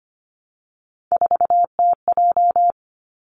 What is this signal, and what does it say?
Morse code, heard as a keyed tone, about 25 wpm, 700 Hz. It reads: 4TJ
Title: Three Challenging Letter-Number Combinations 25wpm